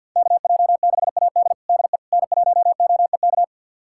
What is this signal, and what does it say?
Morse code, heard as a keyed tone, about 53 wpm, 700 Hz. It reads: KJ6AD DE N1OEK
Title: USA Callsign DE Callsign - 5 Characters - 50wpm